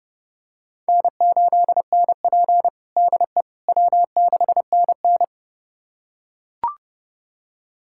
Morse code, heard as a keyed tone, about 30 wpm, 700 Hz. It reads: N8NP DE W6NN
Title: USA Callsign DE Callsign - 4 Characters - 30wpm